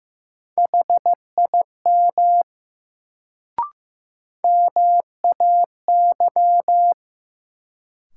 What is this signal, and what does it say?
Morse code, heard as a keyed tone, about 15 wpm, 700 Hz. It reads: HIM MAY
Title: Sentences from Top 100 Words - New Word Drill - 15wpm